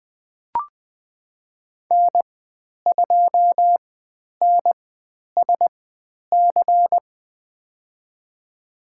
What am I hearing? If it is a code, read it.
Morse code, heard as a keyed tone, about 20 wpm, spaced as for 10 wpm, 700 Hz. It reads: N2NSC
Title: 102 Lesson 17g - Letter u - Call Signs 20wpm